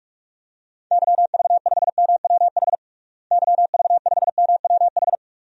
Morse code, heard as a keyed tone, about 46 wpm, 700 Hz. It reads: YV5MWH YV5MWH
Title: DXCC-100 Call Signs - 6 Characters - Repeated 2x - 45wpm